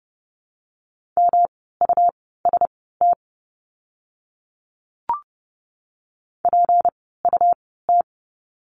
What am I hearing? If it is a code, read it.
Morse code, heard as a keyed tone, about 30 wpm, spaced as for 17 wpm, 700 Hz. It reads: MUST PUT
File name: course / 30:10wpm / 237 Lesson 38f - Letter q - Words 30wpm